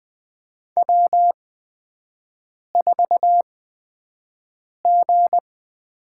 Morse code, heard as a keyed tone, about 20 wpm, spaced as for 5 wpm, 700 Hz. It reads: W4G
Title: USA Call Signs - 8x Character Spacing - 20wpm